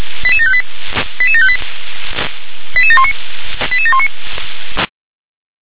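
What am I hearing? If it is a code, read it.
Five-tone selective call (ZVEI), heard as quick runs of five tones, tones of about 70 ms. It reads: 88758 88758 88719 88719
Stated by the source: Five-tone ZVEI selective call 88759 received by SDR.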